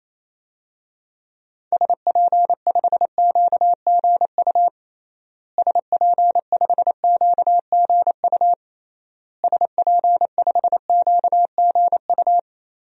Morse code, heard as a keyed tone, about 28 wpm, 700 Hz. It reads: SP5QGU SP5QGU SP5QGU
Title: DXCC-100 Call Signs - 6 Characters - Repeated 3x - 28wpm